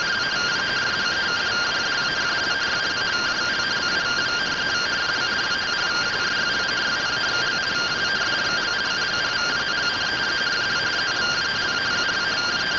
Two-tone teleprinter signal, about 170 Hz shift.